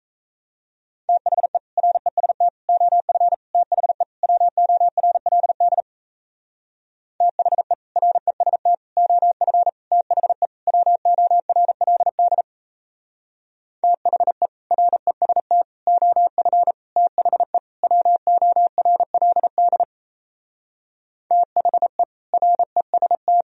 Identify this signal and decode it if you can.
Morse code, heard as a keyed tone, about 36 wpm, 700 Hz. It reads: THE REST OF THE WORLD THE REST OF THE WORLD THE REST OF THE WORLD THE REST